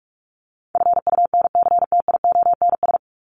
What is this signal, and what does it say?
Morse code, heard as a keyed tone, about 44 wpm, 700 Hz. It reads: FUNCTIONS
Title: ES:S Suffix - Top 100 Words with Base Words 45wpm